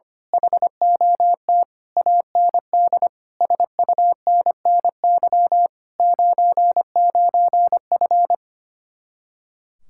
Morse code, heard as a keyed tone, about 24 wpm, 700 Hz. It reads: HOT AND SUNNY 99F
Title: Speed-Racing Common QSO Phrases Encore - 25wpm